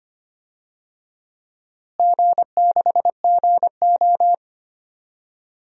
Morse code, heard as a keyed tone, about 25 wpm, 700 Hz. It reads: G6GO
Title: DXCC-100 Call Signs - 4 Characters - Repeated 2x - 25wpm